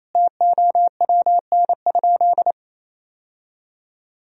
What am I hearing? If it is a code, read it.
Morse code, heard as a keyed tone, about 28 wpm, 700 Hz. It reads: TOWN?
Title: Sentences from Top 1000 Words - Review Part 2 - 28wpm